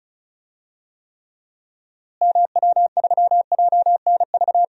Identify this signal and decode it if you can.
Morse code, heard as a keyed tone, about 34 wpm, 700 Hz. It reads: MW3JNV
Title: DXCC-100 Call Signs - 6 Characters - Repeated 3x - 35wpm